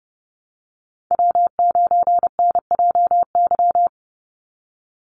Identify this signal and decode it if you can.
Morse code, heard as a keyed tone, about 29 wpm, 700 Hz. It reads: W9NJY
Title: Call Signs - 30wpm